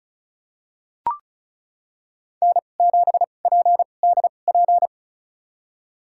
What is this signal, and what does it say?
Morse code, heard as a keyed tone, about 36 wpm, spaced as for 25 wpm, 700 Hz. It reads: N7PDP
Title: USA Call Signs - 5 Characters - 2x Character Spacing - 35wpm